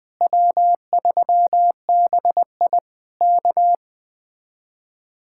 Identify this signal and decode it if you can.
Morse code, heard as a keyed tone, about 20 wpm, 700 Hz. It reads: W3BI K
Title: Speed-Racing Common QSO Phrases - 17wpm